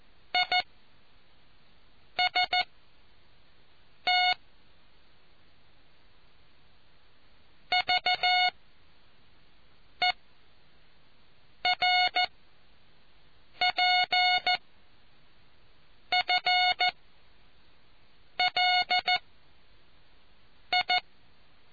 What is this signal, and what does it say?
Morse code, heard as a keyed tone, about 14 wpm, spaced as for 5 wpm, 730 Hz. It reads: IST VERPFLI